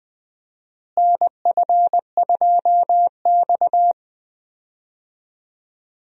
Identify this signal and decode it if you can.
Morse code, heard as a keyed tone, about 20 wpm, 700 Hz. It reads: NF2X
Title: USA Call Signs - 4 Characters - Repeated 2x - 20wpm